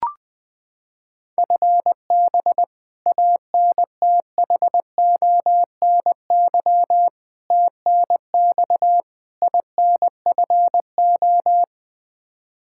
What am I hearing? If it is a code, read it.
Morse code, heard as a keyed tone, about 20 wpm, 700 Hz. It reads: FB ANTHONY TNX INFO
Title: K1USN SST QSO Practice 20wpm